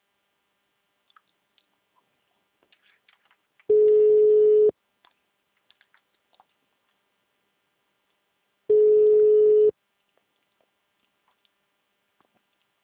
Ringback tone (Europe, 425 Hz).